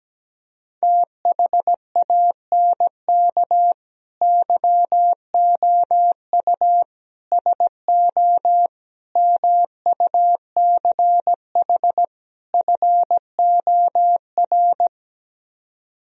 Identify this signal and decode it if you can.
Morse code, heard as a keyed tone, about 17 wpm, 700 Hz. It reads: THANK YOU SO MUCH FOR
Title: Speed-Racing - Sets of 5 Words 17wpm